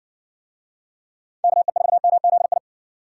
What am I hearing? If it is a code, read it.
Morse code, heard as a keyed tone, about 59 wpm, 700 Hz. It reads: KE4MZI